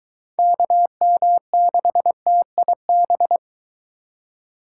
Morse code, heard as a keyed tone, about 23 wpm, 700 Hz. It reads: KM6TIB